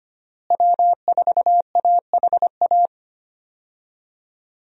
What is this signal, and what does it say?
Morse code, heard as a keyed tone, about 25 wpm, 700 Hz. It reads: W4AHA
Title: USA Call Signs - 5 Characters - Repeated 3x - 25wpm